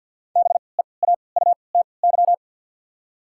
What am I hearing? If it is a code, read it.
Morse code, heard as a keyed tone, about 47 wpm, spaced as for 28 wpm, 700 Hz. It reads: BEAUTY
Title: 3x Character Spacing - Top 500 Words - 50wpm